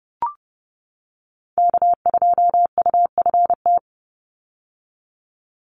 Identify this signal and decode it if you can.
Morse code, heard as a keyed tone, about 30 wpm, 700 Hz. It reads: K2UFT
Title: Call Signs - 30wpm